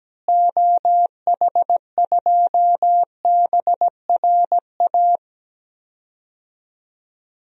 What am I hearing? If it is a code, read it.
Morse code, heard as a keyed tone, about 17 wpm, 700 Hz. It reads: OH2BRA
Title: DXCC-100 Call Signs - 6 Characters - Repeated 2x - 17wpm